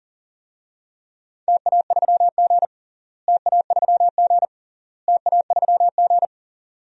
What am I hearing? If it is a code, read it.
Morse code, heard as a keyed tone, about 37 wpm, 700 Hz. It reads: TA3G TA3G TA3G
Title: DXCC-100 Call Signs - 4 Characters - Repeated 3x - 40wpm